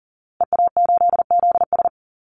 Morse code, heard as a keyed tone, about 40 wpm, 700 Hz. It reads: EA8ZS
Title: Call Signs Repeated 2x - 40wpm